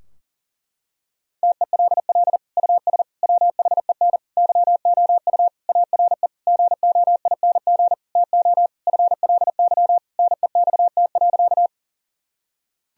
Morse code, heard as a keyed tone, about 40 wpm, 700 Hz. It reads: TELL US WHEN YOU ARE GOING TO FLY NEXT.
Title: Sentences from Top 1000 Words - Review Part 2 - 40wpm